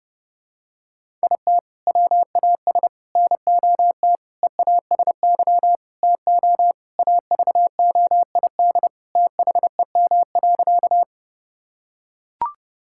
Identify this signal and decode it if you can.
Morse code, heard as a keyed tone, about 30 wpm, 700 Hz. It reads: IT WAS NOT EASY TO AVOID THEM.
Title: Sentences from Top 1000 Words - Review Part 6 - 30wpm